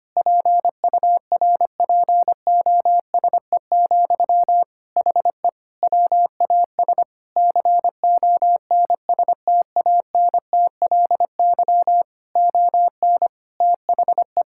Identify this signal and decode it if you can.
Morse code, heard as a keyed tone, about 25 wpm, 700 Hz. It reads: PURPOSE, HE WAS CONSTANTLY ON THE
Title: Sentences from Top 1600 Words 25wpm